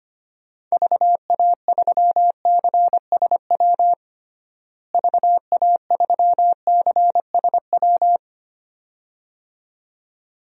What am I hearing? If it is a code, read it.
Morse code, heard as a keyed tone, about 25 wpm, 700 Hz. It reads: VA3CSW VA3CSW
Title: DXCC-100 Call Signs - 6 Characters - Repeated 2x - 25wpm